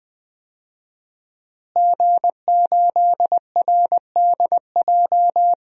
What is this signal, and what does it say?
Morse code, heard as a keyed tone, about 20 wpm, 700 Hz. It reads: G8RDJ
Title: DXCC-100 Call Signs - 5 Characters - Repeated 2x - 20wpm